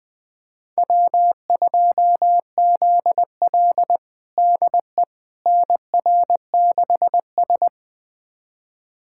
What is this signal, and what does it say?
Morse code, heard as a keyed tone, about 21 wpm, 700 Hz. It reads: W2ZL DE NR6S
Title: USA Callsign DE Callsign - 4 Characters - 20wpm